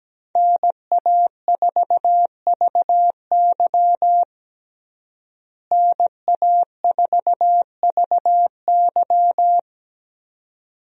Morse code, heard as a keyed tone, about 17 wpm, 700 Hz. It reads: NA4VY NA4VY
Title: USA Call Signs - 5 Characters - Repeated 2x - 17wpm